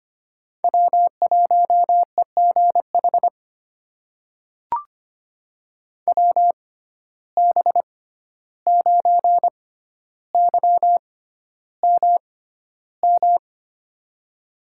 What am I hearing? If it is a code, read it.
Morse code, heard as a keyed tone, about 25 wpm, spaced as for 12 wpm, 700 Hz. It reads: W1EGH WB9YMM
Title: USA Call Signs - 6x Character Spacing - 25wpm